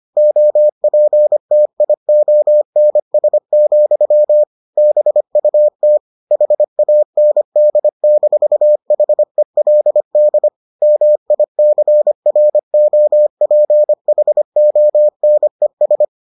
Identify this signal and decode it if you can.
Morse code, heard as a keyed tone, about 25 wpm, 600 Hz. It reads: OPTIONS, BUT HAND-HELD MICROPHONES